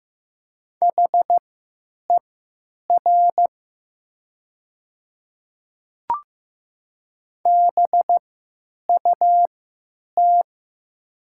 Morse code, heard as a keyed tone, about 15 wpm, spaced as for 9 wpm, 700 Hz. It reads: HER BUT
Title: 3x Character Spacing - Top 100 Words - 15wpm